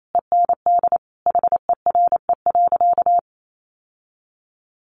Morse code, heard as a keyed tone, about 28 wpm, 700 Hz. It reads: END HERE.
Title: Sentences from Top 1200 Words 28wpm